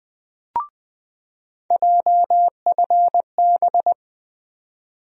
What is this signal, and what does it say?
Morse code, heard as a keyed tone, about 20 wpm, 700 Hz. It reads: JFB
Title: USA Call Sign Postfixes - 3 Letters 20wpm